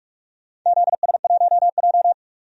Morse code, heard as a keyed tone, about 44 wpm, 700 Hz. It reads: ZS1J
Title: DXCC-100 Call Signs - 4 Characters - 45wpm